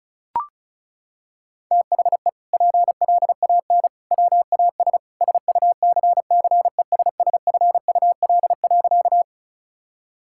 Morse code, heard as a keyed tone, about 34 wpm, 700 Hz. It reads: THE PLAN WAS SUCCESSFUL.
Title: Sentences from Top 1000 Words - Review Part 6 - 35wpm